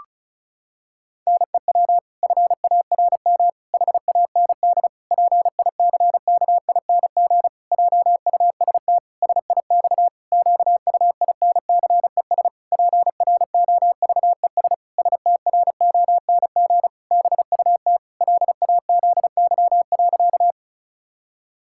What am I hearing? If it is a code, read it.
Morse code, heard as a keyed tone, about 36 wpm, 700 Hz. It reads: NEW FARM HAND PICKING JUST SIX QUINCES PROVES STRONG BUT LAZY.
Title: Pangram Challenge v2 - 35wpm